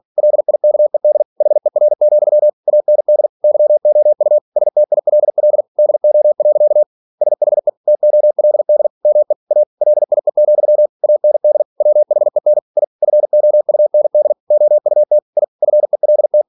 Morse code, heard as a keyed tone, about 46 wpm, 600 Hz. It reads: LIKED HER, AND YOU STILL DO. SHE TOLD ME A LIE, AND WHEN I FOUND OUT I FELT
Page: Never 47wpm